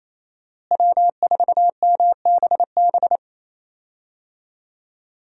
Morse code, heard as a keyed tone, about 28 wpm, 700 Hz. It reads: W4MBB
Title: USA Call Signs - 5 Characters - Repeated 2x - 28wpm